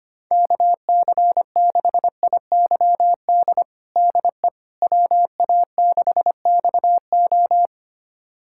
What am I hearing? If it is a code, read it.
Morse code, heard as a keyed tone, about 25 wpm, 700 Hz. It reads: KC6IYD DE WA6XO
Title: USA Callsign DE Callsign 25wpm